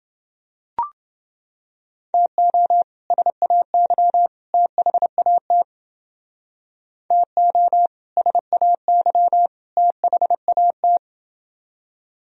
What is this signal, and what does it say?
Morse code, heard as a keyed tone, about 29 wpm, 700 Hz. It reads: TO SAY THAT TO SAY THAT